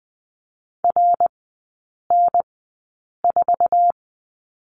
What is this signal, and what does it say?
Morse code, heard as a keyed tone, about 20 wpm, spaced as for 8 wpm, 700 Hz. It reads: RN4
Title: 138 Lesson 23e - Letter f - Three Characters 20wpm